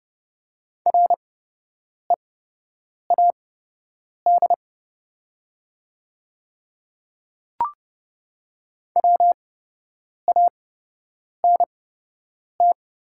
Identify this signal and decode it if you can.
Morse code, heard as a keyed tone, about 29 wpm, spaced as for 8 wpm, 700 Hz. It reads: READ WANT
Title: 8x Character Spacing - Top 200 Words - 30wpm